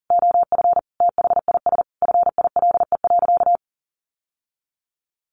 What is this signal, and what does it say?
Morse code, heard as a keyed tone, about 40 wpm, 700 Hz. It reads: OF THIS FILE.
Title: Sentences from Top 1200 Words 40wpm